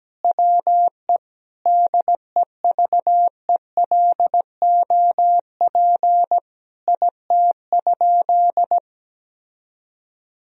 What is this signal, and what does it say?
Morse code, heard as a keyed tone, about 17 wpm, 700 Hz. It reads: WE DEVELOP IT?
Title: Sentences from Top 1000 Words - Review Part 2 - 17wpm